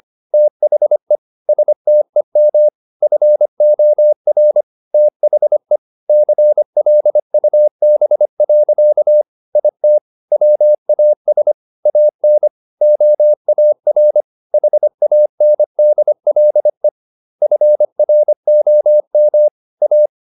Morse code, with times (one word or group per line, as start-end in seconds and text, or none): 0.3-1.2 THE
1.5-2.7 STEM
3.0-4.6 FOR
4.9-5.8 THE
6.1-9.2 CLUB.
9.6-10.0 IT
10.3-11.5 WAS
11.9-12.5 AN
12.8-14.2 OAR
14.5-16.9 HANDLE
17.4-19.5 FROM
19.8-20.1 A